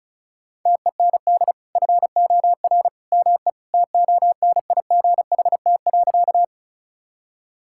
Morse code, heard as a keyed tone, about 34 wpm, 700 Hz. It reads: TEND FOR ME TONIGHT.
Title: Sentences from Top 1000 Words - Review Part 5 - 35wpm